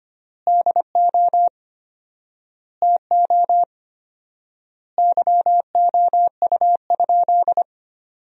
Morse code, heard as a keyed tone, about 25 wpm, 700 Hz. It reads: DO TO YOU?
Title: 4x Word-Spacing - Sentences from Top 100 Words - 25wpm